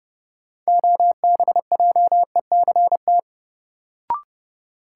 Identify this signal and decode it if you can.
Morse code, heard as a keyed tone, about 30 wpm, 700 Hz. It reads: OBJECT
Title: Sentences from Top 500 Words - New Word Drill - 30wpm